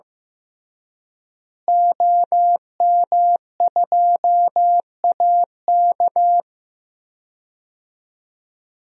Morse code, heard as a keyed tone, about 15 wpm, 700 Hz. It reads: OM2AK
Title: DXCC-100 Call Signs - 5 Characters - Repeated 2x - 15wpm